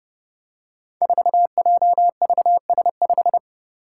Morse code, heard as a keyed tone, about 30 wpm, 700 Hz. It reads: 4JVS5
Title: Five Challenging Letter-Number Combinations 30wpm